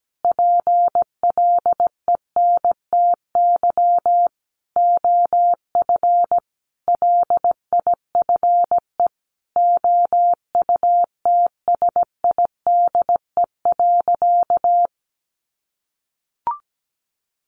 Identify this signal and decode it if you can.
Morse code, heard as a keyed tone, about 17 wpm, 700 Hz. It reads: PLENTY OF LIFE OUTSIDE.
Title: Sentences from Top 1300 Words 17wpm